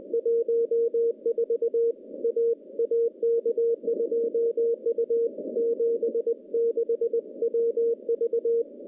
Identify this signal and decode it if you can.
Morse code, heard as a keyed tone, about 20 wpm, 460 Hz. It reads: 14 A AK2U 76 WV
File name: contests / CW-SS-k5tr- -0400